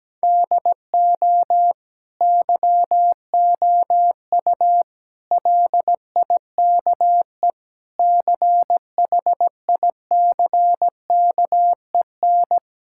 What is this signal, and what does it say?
Morse code, heard as a keyed tone, about 17 wpm, 700 Hz. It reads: DO YOU LIKE CHICKEN
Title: Sentences from Top 1000 Words - Review Part 6 - 17wpm